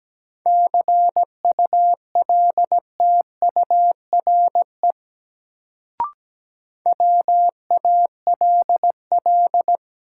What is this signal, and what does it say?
Morse code, heard as a keyed tone, about 17 wpm, 700 Hz. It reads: CULTURE WALL
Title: Sentences from Top 1000 Words - Review Part 6 - New Word Drill - 17wpm